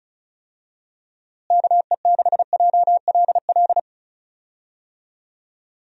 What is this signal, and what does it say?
Morse code, heard as a keyed tone, about 34 wpm, 700 Hz. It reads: KE6JLL
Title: USA Call Signs - 6 Characters - Repeated 3x - 35wpm